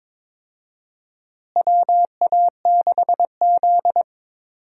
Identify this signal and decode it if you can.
Morse code, heard as a keyed tone, about 22 wpm, 700 Hz. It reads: WA6Z
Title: USA Call Signs - 4 Characters - Repeated 3x - 22wpm